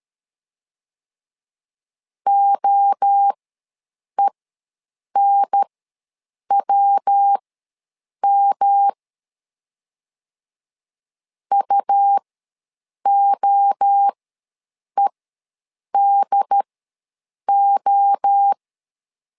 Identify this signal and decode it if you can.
Morse code, heard as a keyed tone, about 13 wpm, spaced as for 7 wpm, 800 Hz. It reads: OENWM UOEDO
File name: LEZIONI / 3 MWRO / 3 MWRO ws 12wpm 1914